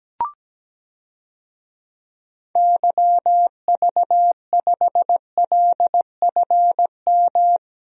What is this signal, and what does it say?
Morse code, heard as a keyed tone, about 17 wpm, 700 Hz. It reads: YV5LFM
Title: DXCC-100 Call Signs - 6 Characters - 17wpm